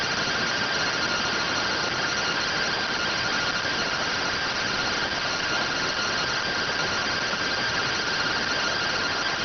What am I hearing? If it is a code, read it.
Two-tone teleprinter signal, about 170 Hz shift.